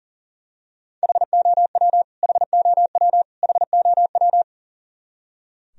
Morse code, heard as a keyed tone, about 40 wpm, 700 Hz. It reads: HOW HOW HOW
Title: Top 100 Words Repeated 40wpm